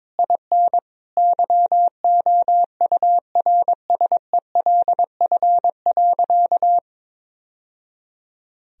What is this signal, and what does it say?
Morse code, heard as a keyed tone, about 22 wpm, 700 Hz. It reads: IN YOURSELF.
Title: Sentences from Top 1000 Words - Review Part 1 - 22wpm